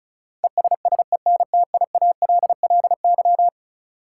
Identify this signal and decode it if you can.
Morse code, heard as a keyed tone, about 34 wpm, 700 Hz. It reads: ESSENTIALLY
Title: Sentences from Top 1300 Words - New Word Drill Alt - 35wpm